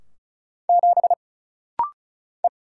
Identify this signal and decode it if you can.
Morse code, heard as a keyed tone, about 33 wpm, 700 Hz. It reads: 7 E
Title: Single Letter-Number - Rapid-Fire 35wpm